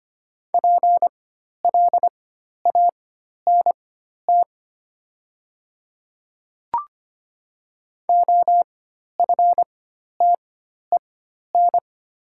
Morse code, heard as a keyed tone, about 25 wpm, spaced as for 12 wpm, 700 Hz. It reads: PLANT OFTEN
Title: 4x Character Spacing - Top 300 Words - 25wpm